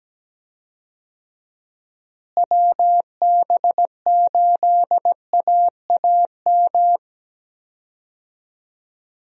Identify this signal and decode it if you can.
Morse code, heard as a keyed tone, about 17 wpm, 700 Hz. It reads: WB8AAM
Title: USA Call Signs - 6 Characters - Repeated 2x - 17wpm